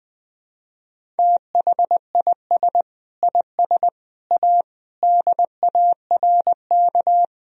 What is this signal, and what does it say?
Morse code, heard as a keyed tone, about 20 wpm, 700 Hz. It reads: THIS IS A DARK
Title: Sentences from Top 1000 Words - Review Part 1 - 20wpm